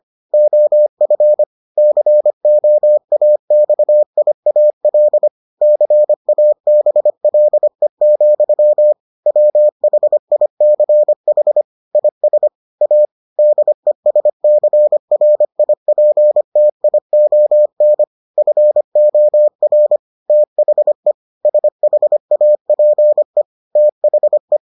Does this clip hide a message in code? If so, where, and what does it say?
Yes: Morse code from 0.3 to 24.6 s, heard as a keyed tone, about 25 wpm, 600 Hz. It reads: OF COAXIAL CABLE, WHICH IS A DESCRIPTION FOR THE SHAPE THE